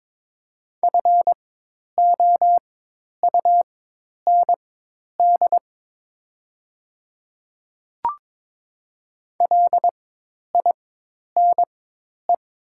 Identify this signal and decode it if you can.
Morse code, heard as a keyed tone, about 22 wpm, spaced as for 10 wpm, 700 Hz. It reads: FOUND LINE
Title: 4x Character Spacing - Top 300 Words - 22wpm